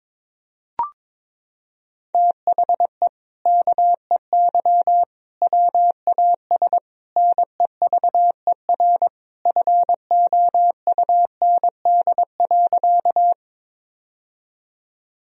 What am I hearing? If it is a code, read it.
Morse code, heard as a keyed tone, about 22 wpm, 700 Hz. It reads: THE KEY WAS NEVER FOUND.
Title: Sentences from Top 1000 Words - Review Part 4 - 22wpm